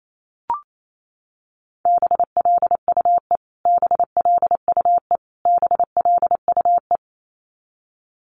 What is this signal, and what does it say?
Morse code, heard as a keyed tone, about 28 wpm, 700 Hz. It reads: BLUE BLUE BLUE
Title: Top 500 Words Repeated 28wpm